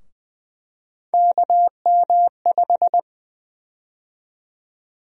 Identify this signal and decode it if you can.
Morse code, heard as a keyed tone, about 20 wpm, 700 Hz. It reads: KM5
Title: USA Call Sign Prefixes - Repeated 2x - 20wpm